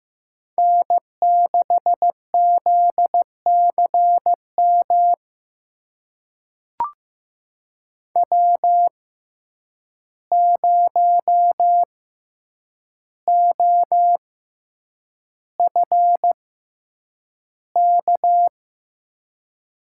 Morse code, heard as a keyed tone, about 15 wpm, spaced as for 8 wpm, 700 Hz. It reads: N6ZCM W0OFK
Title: USA Call Signs - 5 Characters - 6x Character Spacing - 15wpm